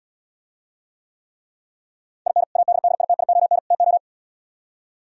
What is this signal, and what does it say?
Morse code, heard as a keyed tone, about 77 wpm, 700 Hz. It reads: IN ADDITION TO